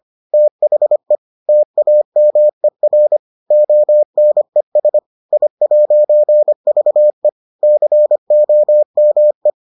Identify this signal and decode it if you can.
Morse code, heard as a keyed tone, about 25 wpm, 600 Hz. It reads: THE TAMER ONES I'VE COME